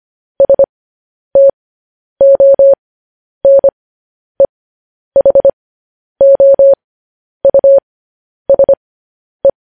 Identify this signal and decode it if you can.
Morse code, heard as a keyed tone, about 25 wpm, spaced as for 10 wpm, 550 Hz. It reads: STONEHOUSE
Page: news headlines in morse code at 10 wpm